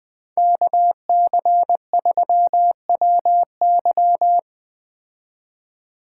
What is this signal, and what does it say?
Morse code, heard as a keyed tone, about 20 wpm, 700 Hz. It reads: KC3WY